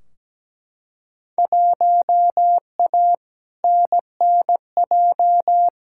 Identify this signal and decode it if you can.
Morse code, heard as a keyed tone, about 17 wpm, 700 Hz. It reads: 1A NNJ
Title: Field Day Exchanges Encore - 17wpm